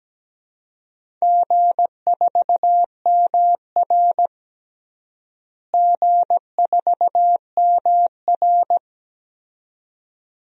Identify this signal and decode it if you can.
Morse code, heard as a keyed tone, about 17 wpm, 700 Hz. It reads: G4MR G4MR
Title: DXCC-100 Call Signs - 4 Characters - Repeated 2x - 17wpm